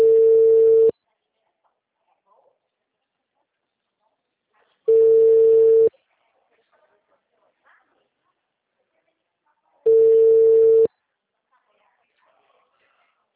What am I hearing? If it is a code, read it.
Ringback tone (China, 450 Hz), with speech.